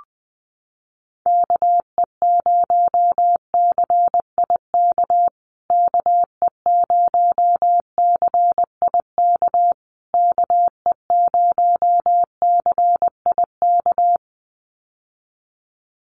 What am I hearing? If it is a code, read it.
Morse code, heard as a keyed tone, about 20 wpm, 700 Hz. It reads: KE0CIK KE0CIK KE0CIK
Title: USA Call Signs Repeated 3x 20wpm